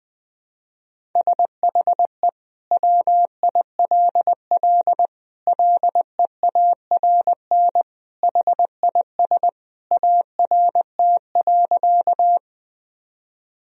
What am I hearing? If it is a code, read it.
Morse code, heard as a keyed tone, about 20 wpm, 700 Hz. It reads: SHE WILL LEARN HIS ART.
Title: Sentences from Top 1000 Words - Review Part 3 - 20wpm